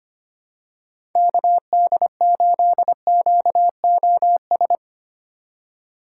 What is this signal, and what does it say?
Morse code, heard as a keyed tone, about 25 wpm, 700 Hz. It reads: KD8QOS
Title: USA Call Signs - 6 Characters - 4x Character Spacing - 25wpm